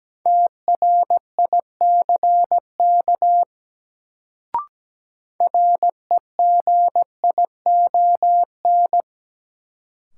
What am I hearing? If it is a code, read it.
Morse code, heard as a keyed tone, about 17 wpm, 700 Hz. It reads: TRICK REGION
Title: Sentences from Top 1500 Words - New Word Drill - 17wpm